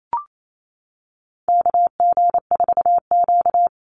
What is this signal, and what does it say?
Morse code, heard as a keyed tone, about 28 wpm, 700 Hz. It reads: KG4Q
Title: Call Signs - 28wpm